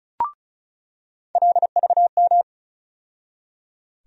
Morse code, heard as a keyed tone, about 34 wpm, 700 Hz. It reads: LVM
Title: USA Call Sign Postfixes - 3 Letters 35wpm